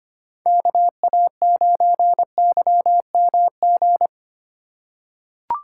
Morse code, heard as a keyed tone, about 25 wpm, 700 Hz. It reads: KA9YMG
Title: K1USN SST QSO Practice - 25wpm